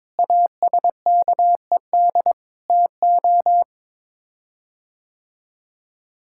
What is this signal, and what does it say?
Morse code, heard as a keyed tone, about 22 wpm, 700 Hz. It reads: ASKED TO
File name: practice / 22wpm / Sets of 6 Words - 22wpm